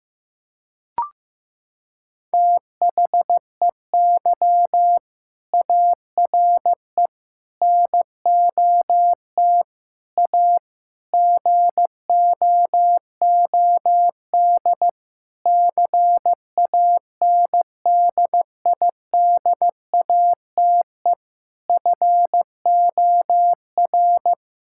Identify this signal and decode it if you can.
Morse code, heard as a keyed tone, about 16 wpm, 700 Hz. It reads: THEY ARE NOT A GOOD CANDIDATE FOR
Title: Sentences from Top 1600 Words 15wpm